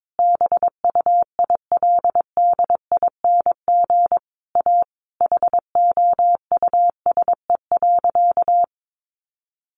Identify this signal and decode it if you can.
Morse code, heard as a keyed tone, about 22 wpm, 700 Hz. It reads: BUILDING A HOUSE.
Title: Sentences from Top 700 Words 22wpm